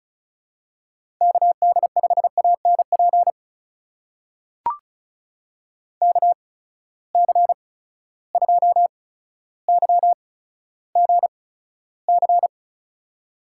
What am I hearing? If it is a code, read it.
Morse code, heard as a keyed tone, about 34 wpm, spaced as for 15 wpm, 700 Hz. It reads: KD5ANP KC2YGC
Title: USA Call Signs - 6 Characters - 8x Character Spacing - 35wpm